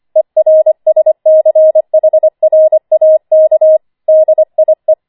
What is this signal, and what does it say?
Morse code, heard as a keyed tone, about 25 wpm, 630 Hz. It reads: ERSCHRAK DIE
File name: 25wpm